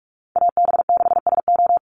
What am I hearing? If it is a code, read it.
Morse code, heard as a keyed tone, about 44 wpm, 700 Hz. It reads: AB6SO
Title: NA SOTA Chaser Callsigns 45wpm